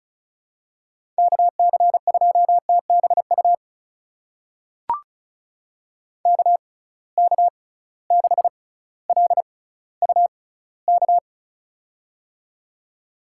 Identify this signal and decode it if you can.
Morse code, heard as a keyed tone, about 34 wpm, spaced as for 18 wpm, 700 Hz. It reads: KC2TBU KK6LUK
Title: USA Call Signs - 6 Characters - 6x Character Spacing - 35wpm